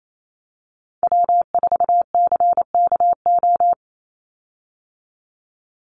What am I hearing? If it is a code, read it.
Morse code, heard as a keyed tone, about 28 wpm, 700 Hz. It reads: W4CKO
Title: USA Call Signs 28wpm